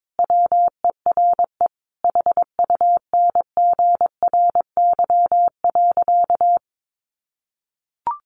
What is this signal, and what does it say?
Morse code, heard as a keyed tone, about 22 wpm, 700 Hz. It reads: WERE HUNGRY.
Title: Sentences from Top 1400 Words 22wpm